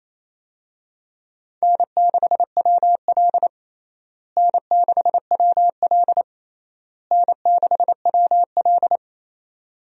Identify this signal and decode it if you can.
Morse code, heard as a keyed tone, about 28 wpm, 700 Hz. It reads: N6WL N6WL N6WL
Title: USA Call Signs - 4 Characters - Repeated 3x - 28wpm